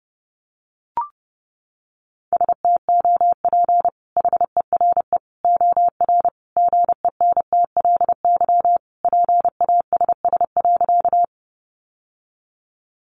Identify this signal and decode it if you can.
Morse code, heard as a keyed tone, about 30 wpm, 700 Hz. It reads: STOP HERE OR GENTLY PASS.
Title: Sentences from Top 1500 Words 30wpm